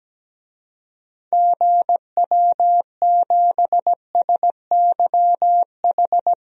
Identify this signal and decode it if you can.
Morse code, heard as a keyed tone, about 17 wpm, 700 Hz. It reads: GW7SYH
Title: DXCC-100 Call Signs - 6 Characters - 17wpm